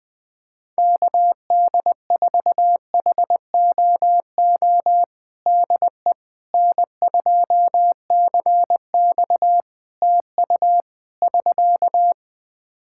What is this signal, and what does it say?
Morse code, heard as a keyed tone, about 20 wpm, 700 Hz. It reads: KD4HOO DE N2CX TU <SK>
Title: Practice QSO - Exchange 4 - Ending - 20wpm